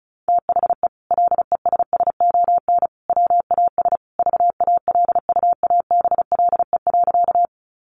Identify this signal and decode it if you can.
Morse code, heard as a keyed tone, about 34 wpm, 700 Hz. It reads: THE LESSON WAS VALUABLE.
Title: Sentences from Top 1200 Words 35wpm